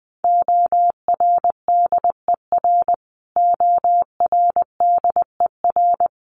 Morse code, heard as a keyed tone, about 20 wpm, 700 Hz. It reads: ORDER ORDER
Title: Top 500 Words Repeated 20wpm